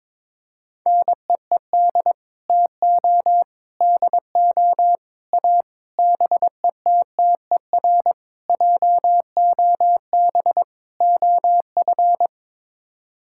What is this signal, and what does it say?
Morse code, heard as a keyed tone, about 22 wpm, 700 Hz. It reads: NEED TO DO A BETTER JOB OF
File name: practice / 22wpm / Sets of 7 Words - 22wpm